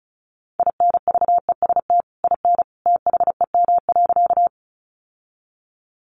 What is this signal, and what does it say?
Morse code, heard as a keyed tone, about 34 wpm, 700 Hz. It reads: INVEST IN THEM.
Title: Sentences from Top 1100 Words 35wpm